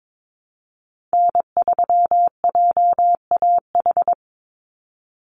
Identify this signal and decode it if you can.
Morse code, heard as a keyed tone, about 22 wpm, 700 Hz. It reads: N3JAH
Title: USA Call Signs 22wpm